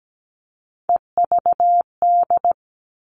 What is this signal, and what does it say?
Morse code, heard as a keyed tone, about 17 wpm, 700 Hz. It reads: EVD
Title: All Cut Numbers - 3-Digit 17wpm